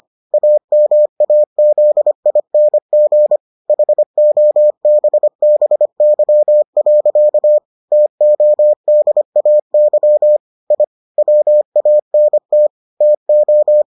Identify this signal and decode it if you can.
Morse code, heard as a keyed tone, about 25 wpm, 600 Hz. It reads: AMAZING HOBBY. TODAY I WANT TO